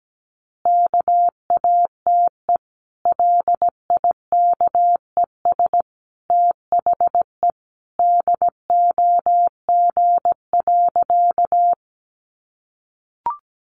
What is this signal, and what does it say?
Morse code, heard as a keyed tone, about 17 wpm, 700 Hz. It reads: KATE LIKES THE DOG.
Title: Easy Phrases 17wpm